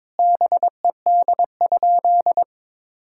Morse code, heard as a keyed tone, about 22 wpm, 700 Hz. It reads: BED?
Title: Sentences from Top 1000 Words - Review Part 6 - 22wpm